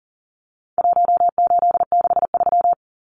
Morse code, heard as a keyed tone, about 37 wpm, 700 Hz. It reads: 1863
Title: Numbers - Contest Style - 4-Digit 40wpm